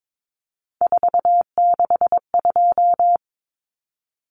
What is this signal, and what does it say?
Morse code, heard as a keyed tone, about 22 wpm, 700 Hz. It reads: 462
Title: Common Cut Numbers - 3-Digit 22wpm